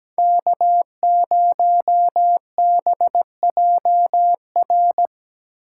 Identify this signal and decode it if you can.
Morse code, heard as a keyed tone, about 17 wpm, 700 Hz. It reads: K0BJR
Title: USA Call Signs - 4x Character Spacing - 17wpm